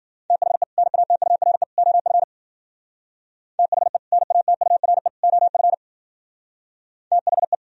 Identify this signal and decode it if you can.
Morse code, heard as a keyed tone, about 47 wpm, 700 Hz. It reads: THE NATURE OF THE NATURE OF THE
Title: Speed-Racing - Sets of 3 Words 40wpm